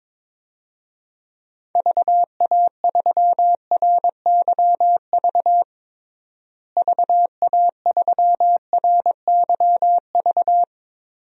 Morse code, heard as a keyed tone, about 22 wpm, 700 Hz. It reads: VA3RYV VA3RYV
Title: DXCC-100 Call Signs - 6 Characters - Repeated 2x - 22wpm